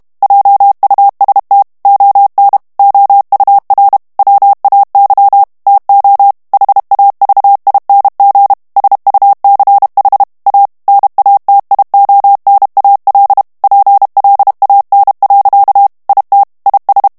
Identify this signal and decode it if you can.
Morse code, heard as a keyed tone, about 32 wpm, 800 Hz. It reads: JUST ON OUR WAY TO HAVING SUCH A NATIONAL PLAN. IT IS
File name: ANGLAIS texte clair 2 vitesse 30 mots minute